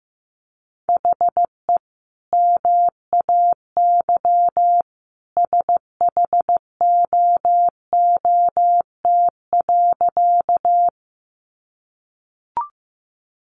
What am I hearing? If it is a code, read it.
Morse code, heard as a keyed tone, about 15 wpm, 700 Hz. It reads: HE MAY SHOOT.
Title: Sentences from Top 1200 Words 15wpm